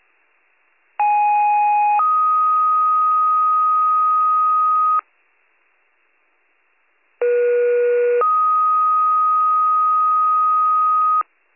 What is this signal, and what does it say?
Two-tone page, heard as a one-second tone, then a three-second tone, about 830 then 1250 Hz, about 480 then 1250 Hz. It is heard from 1.0 to 11.2 s.